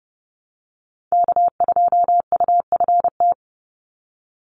Morse code, heard as a keyed tone, about 30 wpm, 700 Hz. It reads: K2UFT
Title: Call Signs Repeated 3x - 30wpm